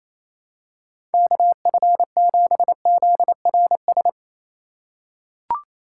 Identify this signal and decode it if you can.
Morse code, heard as a keyed tone, about 28 wpm, 700 Hz. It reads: KF7ZRS
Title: USA Call Signs - 6 Characters - 2x Character Spacing - 28wpm